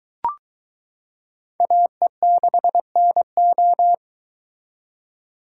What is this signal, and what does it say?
Morse code, heard as a keyed tone, about 22 wpm, 700 Hz. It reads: AE6NO